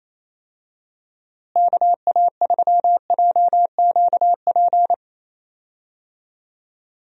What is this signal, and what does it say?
Morse code, heard as a keyed tone, about 28 wpm, 700 Hz. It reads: KA3JQP
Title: USA Call Signs - 6 Characters - Repeated 3x - 28wpm